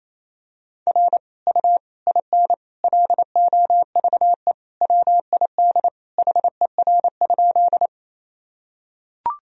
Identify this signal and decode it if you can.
Morse code, heard as a keyed tone, about 27 wpm, 700 Hz. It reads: R U IN LOVE WID HER?
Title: Sentences from Top 400 Words and Abbreviations 28wpm